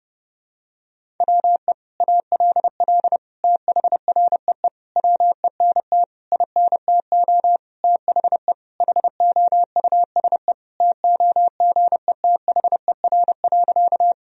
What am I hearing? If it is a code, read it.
Morse code, heard as a keyed tone, about 32 wpm, 700 Hz. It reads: WE ALL THREE WENT INTO THE HOUSE TOGETHER.
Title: Sentences from Top 1000 Words - Review Part 1 - 30wpm